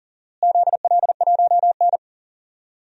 Morse code, heard as a keyed tone, about 40 wpm, 700 Hz. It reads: ZL1N
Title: DXCC-100 Call Signs - 4 Characters - Repeated 3x - 40wpm